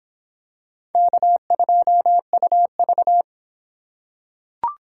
Morse code, heard as a keyed tone, about 26 wpm, 700 Hz. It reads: K2UV